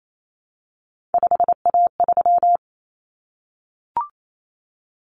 Morse code, heard as a keyed tone, about 28 wpm, 700 Hz. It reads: 5A3
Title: Common Cut Numbers - 3-Digit 28wpm